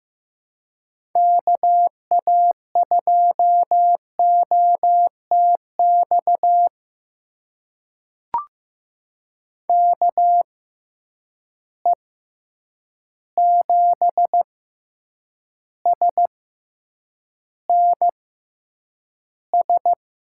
Morse code, heard as a keyed tone, about 15 wpm, spaced as for 8 wpm, 700 Hz. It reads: KA2OTX KE7SNS
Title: USA Call Signs - 6 Characters - 6x Character Spacing - 15wpm